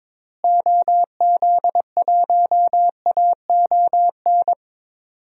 Morse code, heard as a keyed tone, about 22 wpm, 700 Hz. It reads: OZ1AON
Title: DXCC-100 Call Signs - 6 Characters - Repeated 3x - 22wpm